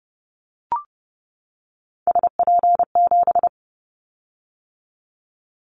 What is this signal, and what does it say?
Morse code, heard as a keyed tone, about 30 wpm, 700 Hz. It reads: SP7
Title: Callsign Prefixes for Top 25 DXCC - 30wpm